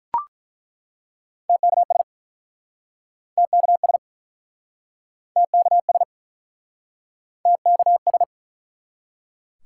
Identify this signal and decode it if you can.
Morse code, heard as a keyed tone, about 49 wpm, 700 Hz. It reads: TKS TKS TKS TKS